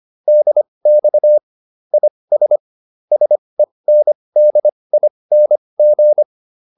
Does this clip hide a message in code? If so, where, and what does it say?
Yes: Morse code from 0.3 to 6.2 s, heard as a keyed tone, about 25 wpm, spaced as for 20 wpm, 600 Hz. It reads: DX IS SENDING